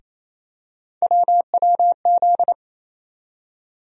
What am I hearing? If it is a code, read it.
Morse code, heard as a keyed tone, about 28 wpm, 700 Hz. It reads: WWZ
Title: USA Call Sign Postfixes - 3 Letters 28wpm